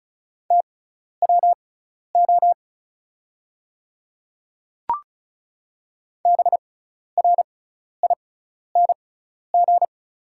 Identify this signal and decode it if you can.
Morse code, heard as a keyed tone, about 34 wpm, spaced as for 12 wpm, 690 Hz. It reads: TWO BRING
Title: 6x Character Spacing - Top 500 Words - 35wpm